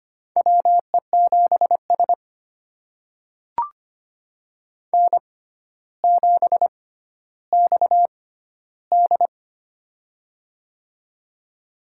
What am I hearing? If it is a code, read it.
Morse code, heard as a keyed tone, about 25 wpm, spaced as for 13 wpm, 700 Hz. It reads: WE7S N7XD
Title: USA Call Signs - 4 Characters - 6x Character Spacing - 25wpm